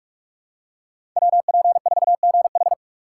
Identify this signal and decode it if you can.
Morse code, heard as a keyed tone, about 46 wpm, 700 Hz. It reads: WP4GH
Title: USA Call Signs - 5 Characters - 3x Character Spacing - 45wpm